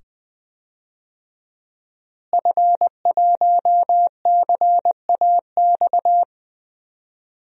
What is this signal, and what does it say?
Morse code, heard as a keyed tone, about 20 wpm, 700 Hz. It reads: F1CAX
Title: DXCC-100 Call Signs - 5 Characters - Repeated 2x - 20wpm